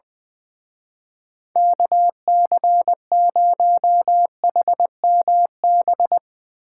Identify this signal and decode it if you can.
Morse code, heard as a keyed tone, about 21 wpm, 700 Hz. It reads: KC0HMB
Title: Speed Racing - USA 2x3 Call Signs 20wpm